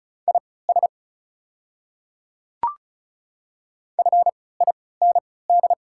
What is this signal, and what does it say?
Morse code, heard as a keyed tone, about 33 wpm, spaced as for 19 wpm, 690 Hz. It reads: IS FIND
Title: 3x Character Spacing - Top 100 Words - 35wpm